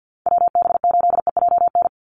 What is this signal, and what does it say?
Morse code, heard as a keyed tone, about 49 wpm, 700 Hz. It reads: WB8EJN
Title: Call Signs - 50wpm